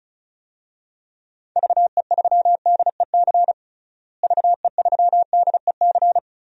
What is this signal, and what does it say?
Morse code, heard as a keyed tone, about 36 wpm, 700 Hz. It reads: VE3DEC VE3DEC
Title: DXCC-100 Call Signs - 6 Characters - Repeated 3x - 35wpm